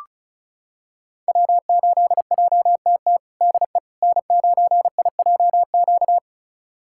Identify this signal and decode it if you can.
Morse code, heard as a keyed tone, about 36 wpm, 700 Hz. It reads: W8JTT DE N9IJQ
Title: USA Callsign DE Callsign - 5 Characters - 35wpm